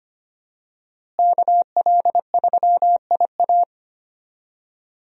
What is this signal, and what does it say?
Morse code, heard as a keyed tone, about 25 wpm, 700 Hz. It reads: KL3IA
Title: USA Call Signs - 5 Characters - Repeated 3x - 25wpm